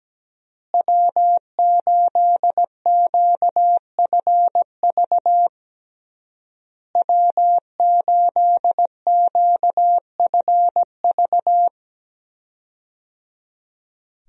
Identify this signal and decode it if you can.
Morse code, heard as a keyed tone, about 17 wpm, 700 Hz. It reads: W8QFV W8QFV
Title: USA Call Signs - 5 Characters - Repeated 3x - 17wpm